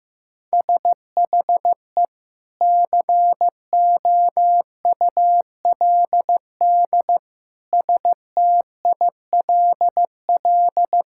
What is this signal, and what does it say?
Morse code, heard as a keyed tone, about 15 wpm, 700 Hz. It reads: SHE COULD STILL
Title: Sentences from Top 1600 Words 15wpm